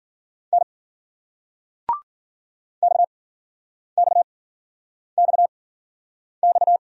Morse code, heard as a keyed tone, about 46 wpm, spaced as for 9 wpm, 700 Hz. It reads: N XXXX
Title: Speed-Racing Single Letters 40wpm